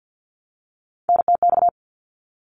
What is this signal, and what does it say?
Morse code, heard as a keyed tone, about 50 wpm, 700 Hz. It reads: NTX
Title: ARRL Sections 50wpm